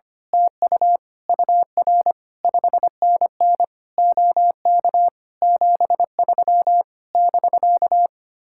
Morse code, heard as a keyed tone, about 24 wpm, 700 Hz. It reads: TU UR 5NN OK 73 <BK>
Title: Speed-Racing Common QSO Phrases - 25wpm